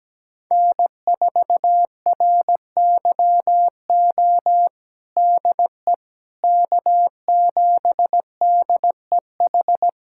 Morse code, heard as a keyed tone, about 17 wpm, 700 Hz. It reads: N4RYO DE K7DEH
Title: USA Callsign DE Callsign - 5 Characters - 17wpm